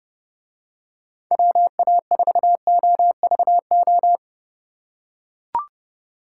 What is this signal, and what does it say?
Morse code, heard as a keyed tone, about 29 wpm, 700 Hz. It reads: WA4OVO